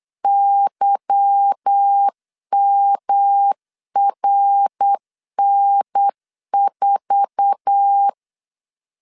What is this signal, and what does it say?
Morse code, heard as a keyed tone, about 8 wpm, 800 Hz. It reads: YMRN4
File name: LEZIONI / 8 YJ45 / 8 YJ4 8wpm 3246